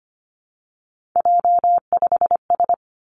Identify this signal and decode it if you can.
Morse code, heard as a keyed tone, about 25 wpm, 700 Hz. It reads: J5S
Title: Three Challenging Letter-Number Combinations 25wpm